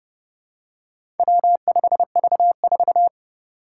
Morse code, heard as a keyed tone, about 30 wpm, 700 Hz. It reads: W5V4
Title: Four Challenging Letter-Number Combinations 30wpm